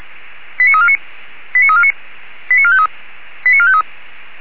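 Five-tone selective call (ZVEI), heard as quick runs of five tones, tones of about 70 ms. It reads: 89349 89349 89453 89453
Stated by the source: Five-tone ZVEI selective call received by SDR.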